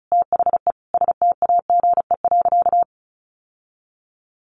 Morse code, heard as a keyed tone, about 34 wpm, 700 Hz. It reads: THE STAGE.
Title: Sentences from Top 1200 Words 35wpm